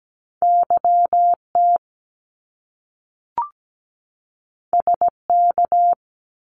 Morse code, heard as a keyed tone, about 17 wpm, 700 Hz. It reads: YT SK
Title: Canadian Provinces and Territories 17wpm